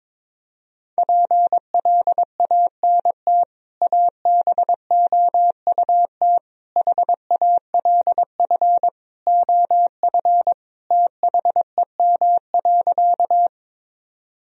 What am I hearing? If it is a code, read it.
Morse code, heard as a keyed tone, about 22 wpm, 700 Hz. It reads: PLANT ABOUT HALF OF THEM.
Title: Sentences from Top 1000 Words - Review Part 1 - 22wpm